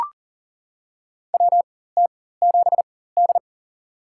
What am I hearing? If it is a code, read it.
Morse code, heard as a keyed tone, about 40 wpm, spaced as for 19 wpm, 700 Hz. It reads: WT7D
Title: USA Call Signs - 4 Characters - 4x Character Spacing - 40wpm